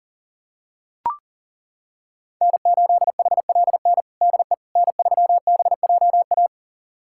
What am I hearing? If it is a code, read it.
Morse code, heard as a keyed tone, about 40 wpm, 700 Hz. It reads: N8HLN DE N3BJA
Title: USA Callsign DE Callsign - 5 Characters - 40wpm